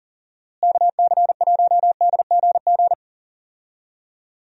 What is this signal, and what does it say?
Morse code, heard as a keyed tone, about 40 wpm, 700 Hz. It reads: KC1DGG
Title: USA Call Signs - 6 Characters - Repeated 2x - 40wpm